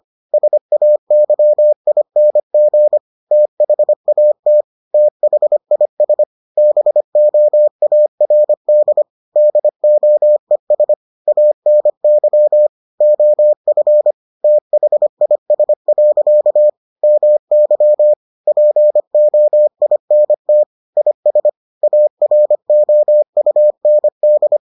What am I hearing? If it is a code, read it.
Morse code, heard as a keyed tone, about 25 wpm, 600 Hz. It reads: SAYING THAT THIS BOARD DOES ANY OF THIS. MY POINT IS AROUND